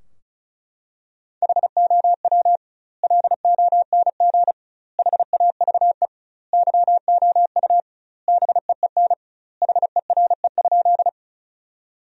Morse code, heard as a keyed tone, about 36 wpm, 700 Hz. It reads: HOW LONG HAVE YOU BEEN HERE?
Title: 2x Word-Spacing - Sentences from Top 200 Words - 35wpm